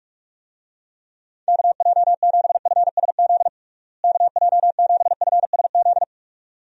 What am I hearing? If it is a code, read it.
Morse code, heard as a keyed tone, about 46 wpm, 700 Hz. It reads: KJ7FSZ KJ7FSZ
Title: USA Call Signs - 6 Characters - Repeated 2x - 45wpm